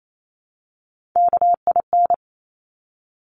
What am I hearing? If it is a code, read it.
Morse code, heard as a keyed tone, about 28 wpm, 700 Hz. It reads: KIN
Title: Top Three Letter Combinations in Words 28wpm